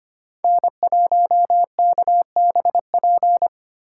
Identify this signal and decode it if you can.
Morse code, heard as a keyed tone, about 25 wpm, 700 Hz. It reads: N1KBP
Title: USA Call Signs - 3x Character Spacing - 25wpm